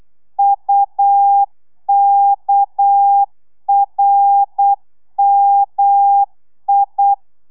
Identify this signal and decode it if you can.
Morse code, heard as a keyed tone, about 8 wpm, 800 Hz. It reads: UKRMI
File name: telegrafia / i2tfj / Lez6 35